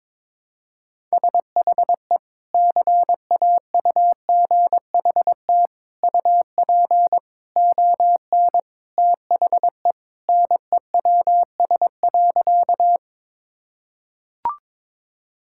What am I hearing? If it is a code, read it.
Morse code, heard as a keyed tone, about 22 wpm, 700 Hz. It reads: SHE CAUGHT UP ON THE NEWS.
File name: practice / 22wpm / Sentences with 3-Word Phrases - 22wpm